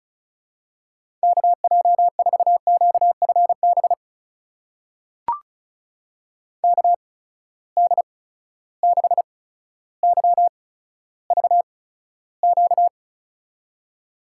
Morse code, heard as a keyed tone, about 36 wpm, spaced as for 15 wpm, 700 Hz. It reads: KJ4QFB KD6YVQ
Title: USA Call Signs - 6 Characters - 8x Character Spacing - 35wpm